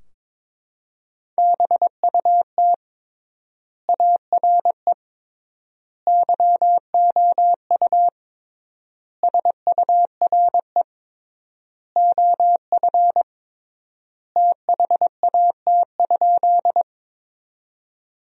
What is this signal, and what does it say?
Morse code, heard as a keyed tone, about 22 wpm, 700 Hz. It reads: BUT ARE YOU SURE OF THAT?
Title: 3x Word-Spacing - Sentences from Top 300 Words - 22wpm